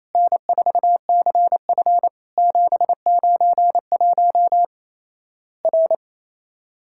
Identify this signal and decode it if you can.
Morse code, heard as a keyed tone, about 27 wpm, 700 Hz. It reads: N4CF 791 R
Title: Worked All Europe DX Contest - QTC Exchange - 28wpm